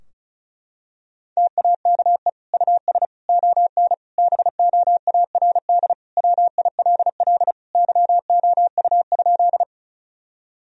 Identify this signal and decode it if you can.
Morse code, heard as a keyed tone, about 34 wpm, 700 Hz. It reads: TAKE US ON BOARD WILL YOU?
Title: Sentences from Top 1000 Words - Review Part 4 - 35wpm